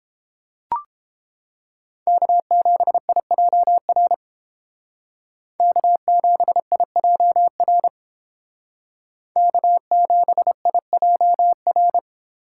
Morse code, heard as a keyed tone, about 30 wpm, 700 Hz. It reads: K7IJR K7IJR K7IJR